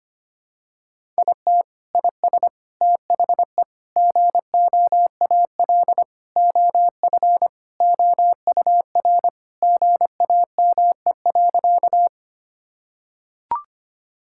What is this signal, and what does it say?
Morse code, heard as a keyed tone, about 25 wpm, 700 Hz. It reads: IT IS THE GOAL OF OUR GAME.
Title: Sentences from Top 1000 Words - Review Part 5 - 25wpm